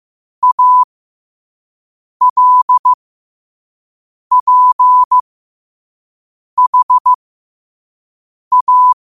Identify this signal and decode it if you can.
Morse code, heard as a keyed tone, about 15 wpm, spaced as for 5 wpm, 1000 Hz. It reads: ALPHA
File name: qotdfiles / 5wpm